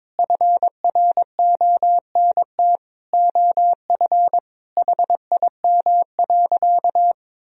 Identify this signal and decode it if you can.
Morse code, heard as a keyed tone, about 21 wpm, 700 Hz. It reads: FRONT OF HIM.
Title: Sentences from Top 1000 Words - Review Part 1 - 22wpm